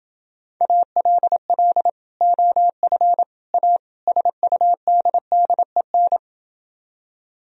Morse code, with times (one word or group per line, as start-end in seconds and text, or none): 0.6-1.9 ALL
2.2-3.2 OF
3.5-3.8 A
4.1-6.2 SUDDEN